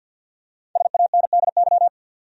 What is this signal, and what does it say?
Morse code, heard as a keyed tone, about 52 wpm, 700 Hz. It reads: SANDY
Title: USA Common Names - 50wpm